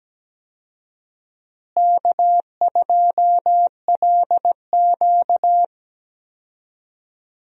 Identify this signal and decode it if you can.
Morse code, heard as a keyed tone, about 17 wpm, 700 Hz. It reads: K2LQ
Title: USA Call Signs - 4 Characters - Repeated 3x - 17wpm